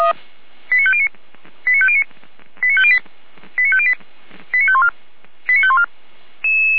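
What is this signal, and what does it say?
Five-tone selective call (ZVEI), heard as quick runs of five tones, tones of about 70 ms. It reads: 89509 89509 89508 89508 89514 89514
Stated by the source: ZVEI five-tone call received by SDR, recorded by Feuerwehr Orlamünde.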